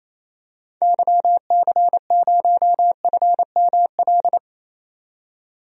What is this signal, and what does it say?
Morse code, heard as a keyed tone, about 28 wpm, 700 Hz. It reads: YC0FML
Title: DXCC-100 Call Signs - 6 Characters - 28wpm